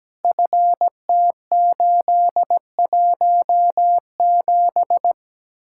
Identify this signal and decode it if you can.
Morse code, heard as a keyed tone, about 17 wpm, 700 Hz. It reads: FT817
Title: Speed-Racing Common QSO Phrases - 17wpm